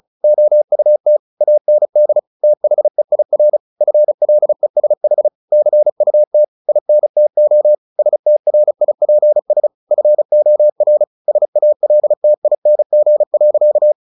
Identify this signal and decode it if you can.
Morse code, heard as a keyed tone, about 34 wpm, 600 Hz. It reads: OUT AND THEIR FLESH CUT INTO STRIPS FOR SALTING.
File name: the old man and the sea / the old man and the sea 1 35wpm